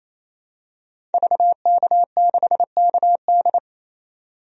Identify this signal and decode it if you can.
Morse code, heard as a keyed tone, about 28 wpm, 700 Hz. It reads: VK6KD
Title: DXCC-100 Call Signs - 5 Characters - 28wpm